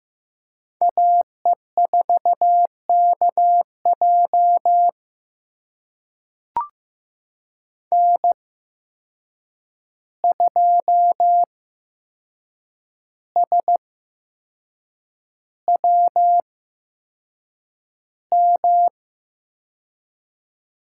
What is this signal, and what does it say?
Morse code, heard as a keyed tone, about 15 wpm, spaced as for 6 wpm, 700 Hz. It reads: AE4KJ N2SWM
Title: USA Call Signs - 5 Characters - 8x Character Spacing - 15wpm